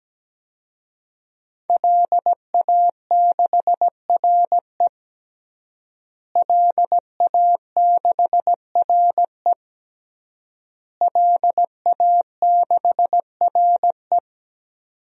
Morse code, heard as a keyed tone, about 17 wpm, 700 Hz. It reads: LA6RE LA6RE LA6RE
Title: DXCC-100 Call Signs - 5 Characters - Repeated 3x - 17wpm